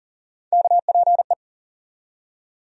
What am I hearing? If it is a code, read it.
Morse code, heard as a keyed tone, about 40 wpm, 700 Hz. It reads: KPE
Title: USA Call Sign Postfixes - 3 Letters 40wpm